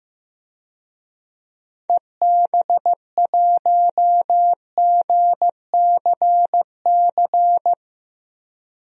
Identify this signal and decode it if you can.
Morse code, heard as a keyed tone, about 15 wpm, 700 Hz. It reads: EB1GCC
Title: DXCC-100 Call Signs - 6 Characters - 15wpm